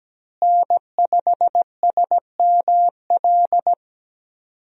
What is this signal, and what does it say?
Morse code, heard as a keyed tone, about 17 wpm, 700 Hz. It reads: N5SML
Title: USA Call Signs - 5 Characters - Repeated 2x - 17wpm